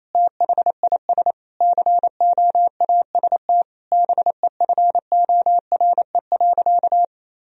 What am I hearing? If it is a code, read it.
Morse code, heard as a keyed tone, about 28 wpm, 700 Hz. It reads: THIS COAST BEFORE.
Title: Sentences from Top 1000 Words - Review Part 4 - 28wpm